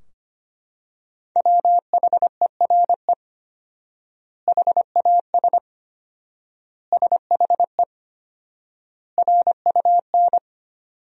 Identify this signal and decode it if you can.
Morse code, heard as a keyed tone, about 25 wpm, 700 Hz. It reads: WHERE HAS SHE RUN
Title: 4x Word-Spacing - Sentences from Top 300 Words - 25wpm